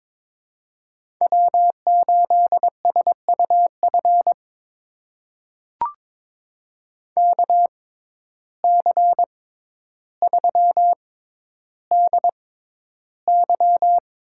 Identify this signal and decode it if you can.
Morse code, heard as a keyed tone, about 21 wpm, spaced as for 11 wpm, 700 Hz. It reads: W8SUF KC3DY
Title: USA Call Signs - 6x Character Spacing - 22wpm